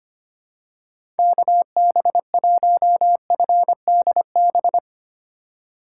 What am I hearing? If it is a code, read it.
Morse code, heard as a keyed tone, about 25 wpm, 700 Hz. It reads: KB1FDB
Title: USA Call Signs - 6 Characters - 3x Character Spacing - 25wpm